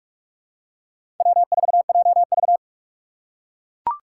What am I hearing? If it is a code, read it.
Morse code, heard as a keyed tone, about 42 wpm, 700 Hz. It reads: W4JV
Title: USA Call Signs - 4 Characters - 4x Character Spacing - 45wpm